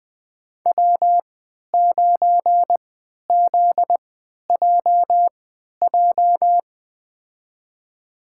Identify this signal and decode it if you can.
Morse code, heard as a keyed tone, about 20 wpm, spaced as for 11 wpm, 700 Hz. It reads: W9ZJJ
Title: USA Call Signs - 5 Characters - 3x Character Spacing - 20wpm